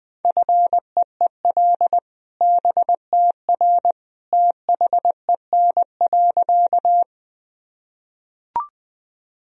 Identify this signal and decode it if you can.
Morse code, heard as a keyed tone, about 20 wpm, 700 Hz. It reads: FEEL BTR THEN.
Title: Sentences from Top 400 Words and Abbreviations 20wpm